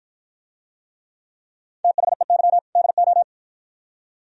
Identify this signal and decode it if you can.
Morse code, heard as a keyed tone, about 48 wpm, 700 Hz. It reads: THEY DO